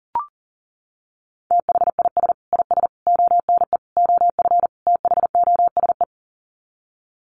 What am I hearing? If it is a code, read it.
Morse code, heard as a keyed tone, about 40 wpm, 700 Hz. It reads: THIS IS ONE OF THOSE
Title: Sets of 5 Words Encore2 40wpm